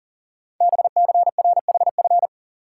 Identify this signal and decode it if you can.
Morse code, heard as a keyed tone, about 40 wpm, 700 Hz. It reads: BCRHF
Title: Code Groups - 5 Letters - 40wpm